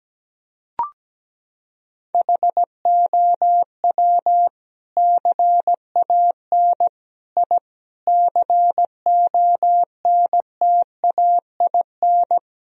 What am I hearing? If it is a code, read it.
Morse code, heard as a keyed tone, about 17 wpm, 700 Hz. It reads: HOW CAN I CONTAIN
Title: Sentences from Top 1000 Words - Review Part 2 - 17wpm